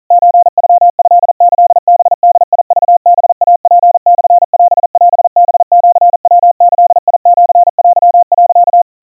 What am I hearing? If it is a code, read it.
Morse code (Japanese Wabun code), heard as a keyed tone, about 40 wpm, 700 Hz. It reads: ソノトキハボクハイツモオオハシヤギシテ、
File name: kawa / Morse sample / kamyu 150 cpm 700 Hz Crispy